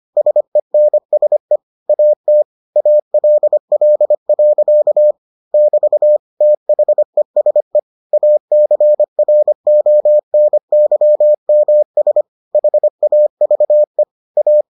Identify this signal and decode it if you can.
Morse code, heard as a keyed tone, about 25 wpm, 600 Hz. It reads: SENSE AT ALL. <BT> THESE ACRONYMS HAVE A